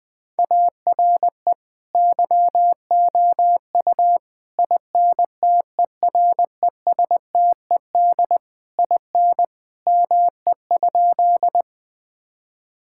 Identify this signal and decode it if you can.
Morse code, heard as a keyed tone, about 19 wpm, 700 Hz. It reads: ARE YOU INTERESTED IN ME?
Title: Sentences from Top 1000 Words - Review Part 6 - 20wpm